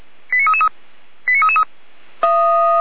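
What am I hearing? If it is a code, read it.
Five-tone selective call (ZVEI), heard as quick runs of five tones, tones of about 70 ms. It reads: 89333 89333
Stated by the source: ZVEI five-tone call received by SDR.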